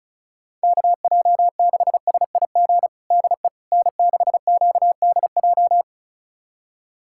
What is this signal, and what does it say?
Morse code, heard as a keyed tone, about 36 wpm, 700 Hz. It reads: KJ6SIG DE N6QDJ
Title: USA Callsign DE Callsign 35wpm